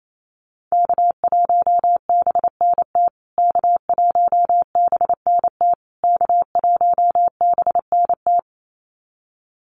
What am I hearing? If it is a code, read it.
Morse code, heard as a keyed tone, about 28 wpm, 700 Hz. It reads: K1BNT K1BNT K1BNT
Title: USA Call Signs Repeated 3x 28wpm